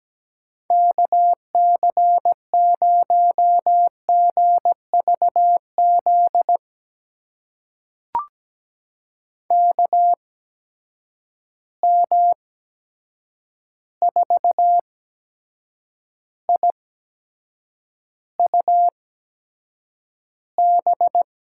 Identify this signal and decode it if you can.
Morse code, heard as a keyed tone, about 17 wpm, spaced as for 7 wpm, 700 Hz. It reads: KC0GVZ KM4IUB
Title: USA Call Signs - 6 Characters - 8x Character Spacing - 17wpm